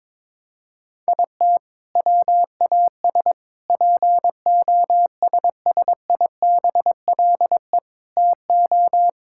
Morse code, heard as a keyed tone, about 21 wpm, 700 Hz. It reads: IT WAS POSSIBLE TO
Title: Sentences from Top 1600 Words 22wpm